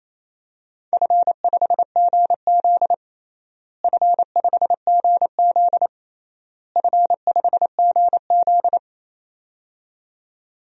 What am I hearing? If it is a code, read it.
Morse code, heard as a keyed tone, about 27 wpm, 700 Hz. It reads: F5GZ F5GZ F5GZ
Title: DXCC-100 Call Signs - 4 Characters - Repeated 3x - 28wpm